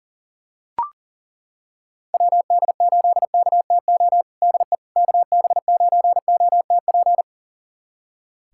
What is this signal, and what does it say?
Morse code, heard as a keyed tone, about 40 wpm, 700 Hz. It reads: WD8KTO DE KB9OTP
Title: USA Callsign DE Callsign - 6 Characters - 40wpm